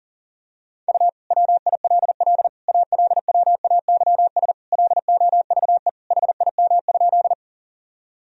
Morse code, heard as a keyed tone, about 37 wpm, 700 Hz. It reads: U WILL ALWAYS LOVE HIM?
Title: Sentences from Top 400 Words and Abbreviations 40wpm